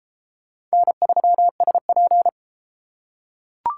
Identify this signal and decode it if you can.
Morse code, heard as a keyed tone, about 31 wpm, 700 Hz. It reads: N3SP